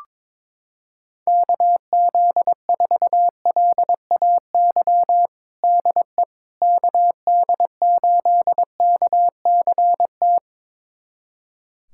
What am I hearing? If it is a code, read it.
Morse code, heard as a keyed tone, about 22 wpm, 700 Hz. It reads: KZ4LAY DE KD8KCT
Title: USA Callsign DE Callsign - 6 Characters - 22wpm